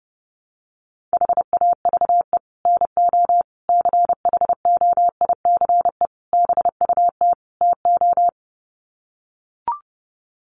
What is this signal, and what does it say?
Morse code, heard as a keyed tone, about 30 wpm, 700 Hz. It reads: HAVE NO CHOICE BUT TO
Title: Sets of 5 Words Encore 30wpm